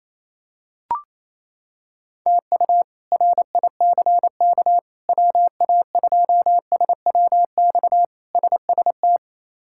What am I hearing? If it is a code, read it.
Morse code, heard as a keyed tone, about 28 wpm, 700 Hz. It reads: TU RICK WA2SWX SST
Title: K1USN SST QSO Practice - 28wpm